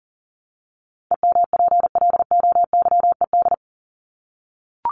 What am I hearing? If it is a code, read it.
Morse code, heard as a keyed tone, about 40 wpm, 700 Hz. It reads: EMPLOYED
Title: ED Suffix - Top 100 Words with Base Words 40wpm